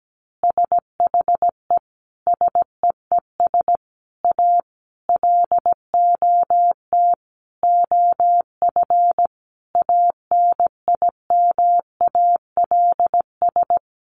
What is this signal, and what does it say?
Morse code, heard as a keyed tone, about 17 wpm, 700 Hz. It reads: SHE SEES A LOT OF ANIMALS
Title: Easy Phrases 17wpm